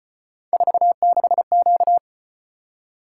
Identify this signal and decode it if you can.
Morse code, heard as a keyed tone, about 33 wpm, 700 Hz. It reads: 46Q